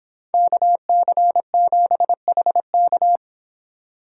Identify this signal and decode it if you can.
Morse code, heard as a keyed tone, about 26 wpm, 700 Hz. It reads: KC7HK